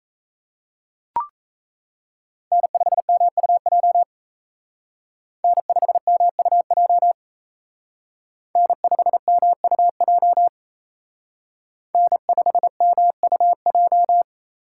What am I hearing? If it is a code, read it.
Morse code, heard as a keyed tone, about 36 wpm, 700 Hz. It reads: N5MUJ N5MUJ N5MUJ N5MUJ